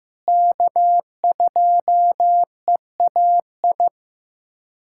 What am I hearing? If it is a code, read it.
Morse code, heard as a keyed tone, about 16 wpm, 700 Hz. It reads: K2EAI
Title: USA Callsign DE Callsign - 5 Characters - 15wpm